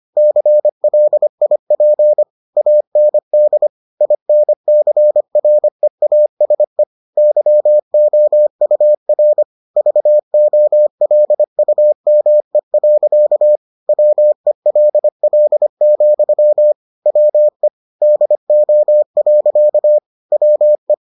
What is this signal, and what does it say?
Morse code, heard as a keyed tone, about 25 wpm, 600 Hz. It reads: CLIP AND INCREASE YOUR VOLUME. WELL, WE DO. WE